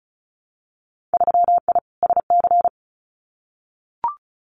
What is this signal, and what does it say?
Morse code, heard as a keyed tone, about 35 wpm, 700 Hz. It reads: 3I SC
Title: Winter Field Day Exchanges Encore 35wpm